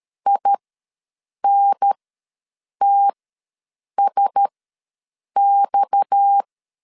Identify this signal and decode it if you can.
Morse code, heard as a keyed tone, about 13 wpm, spaced as for 7 wpm, 800 Hz. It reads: INTSX
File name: LEZIONI / 6 PKX0 / 6 PKX0 ws 12wpm 3448